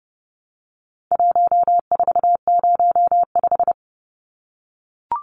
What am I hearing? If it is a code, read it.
Morse code, heard as a keyed tone, about 30 wpm, 700 Hz. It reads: 1405
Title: Numbers - 4-Digit 30wpm